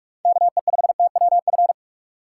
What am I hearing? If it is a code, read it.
Morse code, heard as a keyed tone, about 46 wpm, 700 Hz. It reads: KE5TWF
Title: USA Call Signs - 6 Characters - 4x Character Spacing - 45wpm